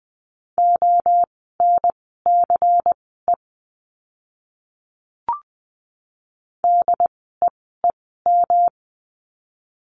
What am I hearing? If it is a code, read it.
Morse code, heard as a keyed tone, about 20 wpm, spaced as for 14 wpm, 700 Hz. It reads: ONCE DEEM
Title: 108 Lesson 18f - Letter m - Words 20wpm